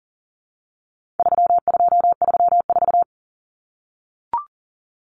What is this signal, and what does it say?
Morse code, heard as a keyed tone, about 40 wpm, 700 Hz. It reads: 3234
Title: Common Cut Numbers - 4-Digit 40wpm